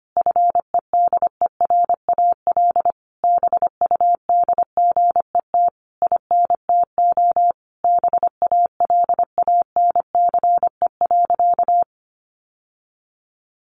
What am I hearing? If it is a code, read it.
Morse code, heard as a keyed tone, about 25 wpm, 700 Hz. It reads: FEDERAL BUDGET INTO BALANCE.
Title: Sentences from Top 1200 Words 25wpm